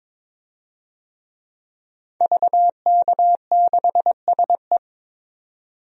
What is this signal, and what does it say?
Morse code, heard as a keyed tone, about 22 wpm, 700 Hz. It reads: VK6SE
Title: DXCC-100 Call Signs - 5 Characters - 22wpm